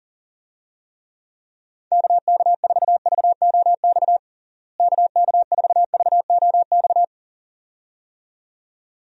Morse code, heard as a keyed tone, about 40 wpm, 700 Hz. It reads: KK4VOX KK4VOX
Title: USA Call Signs - 6 Characters - Repeated 2x - 40wpm